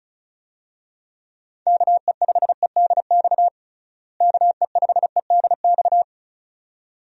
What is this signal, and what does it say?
Morse code, heard as a keyed tone, about 36 wpm, 700 Hz. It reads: KE5EDX KE5EDX
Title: USA Call Signs - 6 Characters - Repeated 2x - 35wpm